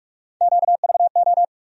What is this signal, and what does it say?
Morse code, heard as a keyed tone, about 46 wpm, 700 Hz. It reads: QVO
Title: USA Call Sign Postfixes - 3 Letters 45wpm